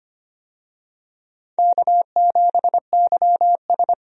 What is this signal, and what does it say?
Morse code, heard as a keyed tone, about 25 wpm, 700 Hz. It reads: K7YS
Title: USA Call Signs - 4 Characters - 25wpm